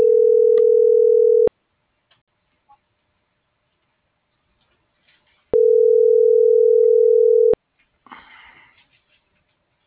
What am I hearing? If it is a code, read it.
Ringback tone (North America), 440 + 480 Hz.